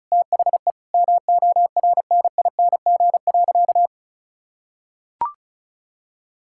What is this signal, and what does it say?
Morse code, heard as a keyed tone, about 36 wpm, 700 Hz. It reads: THE MORNING.
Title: Sentences from Top 1000 Words - Review Part 5 - 35wpm